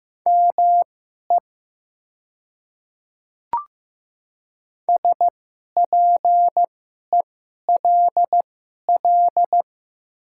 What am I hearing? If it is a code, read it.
Morse code, heard as a keyed tone, about 15 wpm, spaced as for 11 wpm, 700 Hz. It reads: ME SPELL
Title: 2x Character Spacing - Top 500 Words - 15wpm